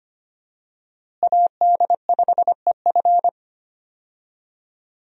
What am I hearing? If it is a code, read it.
Morse code, heard as a keyed tone, about 25 wpm, 700 Hz. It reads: AD5EF
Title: USA Call Signs - 5 Characters - Repeated 2x - 25wpm